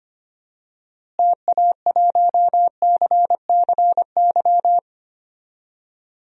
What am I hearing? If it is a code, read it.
Morse code, heard as a keyed tone, about 25 wpm, 700 Hz. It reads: TA1CCY
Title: DXCC-100 Call Signs - 6 Characters - Repeated 2x - 25wpm